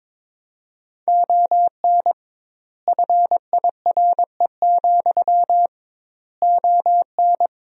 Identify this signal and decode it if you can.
Morse code, heard as a keyed tone, about 21 wpm, 700 Hz. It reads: ON FIRE, ON
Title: 2x Word-Spacing - Sentences from Top 400 Words - 22wpm